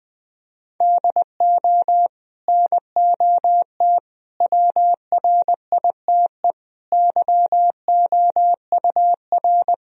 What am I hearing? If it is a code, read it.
Morse code, heard as a keyed tone, about 20 wpm, 700 Hz. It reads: DO NOT WRITE YOUR
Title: Sentences from Top 1000 Words - Review Part 1 - 20wpm